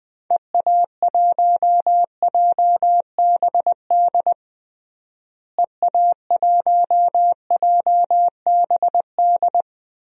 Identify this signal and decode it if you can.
Morse code, heard as a keyed tone, about 20 wpm, 700 Hz. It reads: EA1JBD EA1JBD
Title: DXCC-100 Call Signs - 6 Characters - Repeated 2x - 20wpm